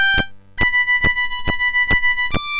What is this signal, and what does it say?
There is no code.